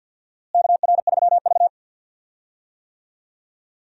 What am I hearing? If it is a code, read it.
Morse code, heard as a keyed tone, about 51 wpm, 700 Hz. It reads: KR3V
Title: USA Call Signs - 4 Characters - Repeated 2x - 50wpm